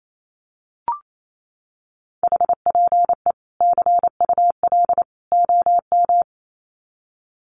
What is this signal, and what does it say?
Morse code, heard as a keyed tone, about 28 wpm, 700 Hz. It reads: HPE CUL OM
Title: Common QSO Phrases 28wpm